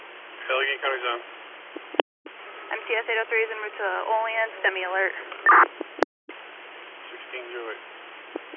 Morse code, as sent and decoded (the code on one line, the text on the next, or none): none